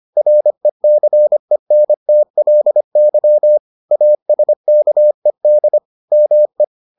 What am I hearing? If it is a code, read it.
Morse code, heard as a keyed tone, about 25 wpm, 600 Hz. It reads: RECENTLY ASKED ME